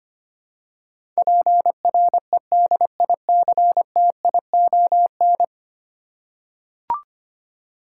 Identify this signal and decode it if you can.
Morse code, heard as a keyed tone, about 25 wpm, 700 Hz. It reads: PREDICTION
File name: Ten Letter Words - 25wpm